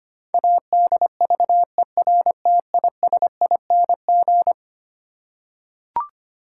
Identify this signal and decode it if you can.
Morse code, heard as a keyed tone, about 25 wpm, 700 Hz. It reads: ADVERTISING
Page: morse-code-ninja-practice-25wpm